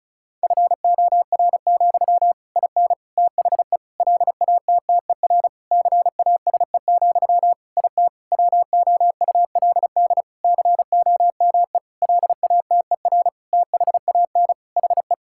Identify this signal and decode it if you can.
Morse code, heard as a keyed tone, about 36 wpm, 700 Hz. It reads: FOR, IN THE LATTER CASE, IT WOULD COME LATER THAN HE
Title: Sentences from Top 1600 Words 35wpm